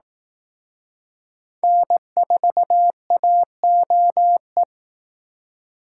Morse code, heard as a keyed tone, about 18 wpm, 700 Hz. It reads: N4AOE
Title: Speed Racing - USA 1x3 Call Signs 15wpm